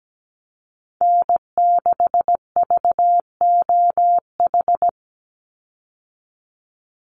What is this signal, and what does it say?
Morse code, heard as a keyed tone, about 17 wpm, 700 Hz. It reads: N6VOH
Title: Call Signs - 17wpm